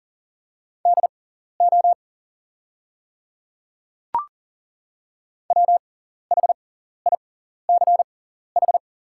Morse code, heard as a keyed tone, about 38 wpm, spaced as for 14 wpm, 690 Hz. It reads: DO WHICH
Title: 6x Character Spacing - Top 100 Words - 40wpm